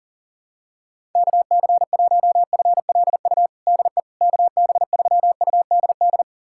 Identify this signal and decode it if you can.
Morse code, heard as a keyed tone, about 40 wpm, 700 Hz. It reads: KC1FLU DE KB3UDD
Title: USA Callsign DE Callsign - 6 Characters - 40wpm